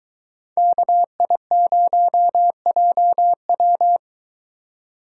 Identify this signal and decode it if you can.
Morse code, heard as a keyed tone, about 22 wpm, 700 Hz. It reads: KI0JW